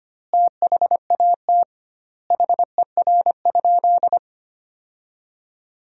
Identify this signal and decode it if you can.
Morse code, heard as a keyed tone, about 25 wpm, 700 Hz. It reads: THAT HER?
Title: 2x Word-Spacing - Sentences from Top 100 Words - 25wpm